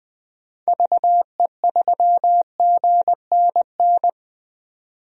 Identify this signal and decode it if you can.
Morse code, heard as a keyed tone, about 21 wpm, 700 Hz. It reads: VE3GNN
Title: DXCC-100 Call Signs - 6 Characters - 20wpm